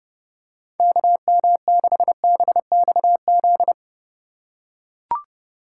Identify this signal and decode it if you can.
Morse code, heard as a keyed tone, about 30 wpm, 700 Hz. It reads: KM6BXZ
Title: USA Call Signs - 6 Characters - 8x Character Spacing - 30wpm